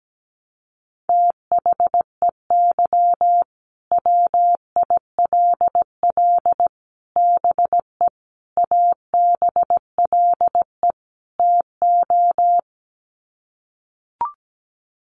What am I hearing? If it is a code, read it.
Morse code, heard as a keyed tone, about 17 wpm, 700 Hz. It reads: THEY WILL BE ABLE TO
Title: Sets of 5 Words Encore 17wpm